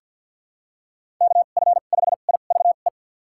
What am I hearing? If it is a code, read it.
Morse code, heard as a keyed tone, about 49 wpm, spaced as for 36 wpm, 700 Hz. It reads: KF5IVE
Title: USA Call Signs - 2x Character Spacing - 50wpm